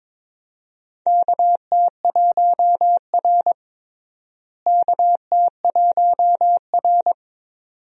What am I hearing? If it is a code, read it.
Morse code, heard as a keyed tone, about 22 wpm, 700 Hz. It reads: KT1R KT1R
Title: USA Call Signs - 4 Characters - Repeated 2x - 22wpm